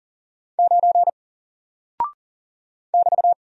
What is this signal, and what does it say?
Morse code, heard as a keyed tone, about 40 wpm, 700 Hz. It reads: 9 <BT>
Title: Single Character (Letter:Number:Punctuation:Prosign) 40wpm